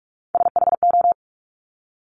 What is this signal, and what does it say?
Morse code, heard as a keyed tone, about 46 wpm, 690 Hz. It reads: SHO
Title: Top Three Letter Combinations in Words 45wpm